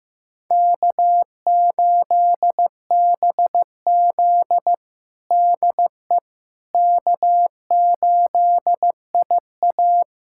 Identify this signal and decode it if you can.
Morse code, heard as a keyed tone, about 15 wpm, 700 Hz. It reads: K8BZ DE K8IA
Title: K1USN SST QSO Practice 15wpm